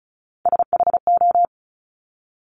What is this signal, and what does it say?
Morse code, heard as a keyed tone, about 36 wpm, 700 Hz. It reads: SHO
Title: Top Three Letter Combinations in Words 35wpm